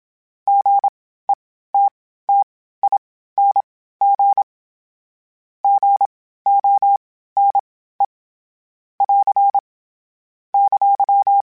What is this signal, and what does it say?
Morse code, heard as a keyed tone, about 26 wpm, spaced as for 15 wpm, 800 Hz. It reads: GETTING GONE <AR> !
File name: Short Runs / Short Runs-mp3s / Gotten @25